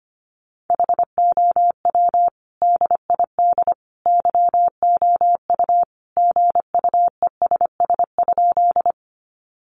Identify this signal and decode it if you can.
Morse code, heard as a keyed tone, about 25 wpm, 700 Hz. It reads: HOW DID YOU GUESS?
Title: Sentences from Top 1500 Words 25wpm